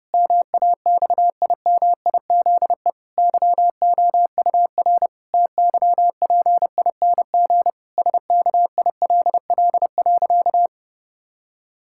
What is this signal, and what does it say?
Morse code, heard as a keyed tone, about 32 wpm, 700 Hz. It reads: MAXIMIZE YOUR TYPING SKILL.
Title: Pangram Challenge v2 - 30wpm